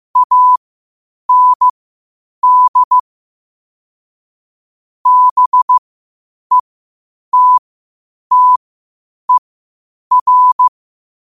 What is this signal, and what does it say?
Morse code, heard as a keyed tone, about 15 wpm, spaced as for 8 wpm, 1000 Hz. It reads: AND BETTER
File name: qotdfiles / 8wpm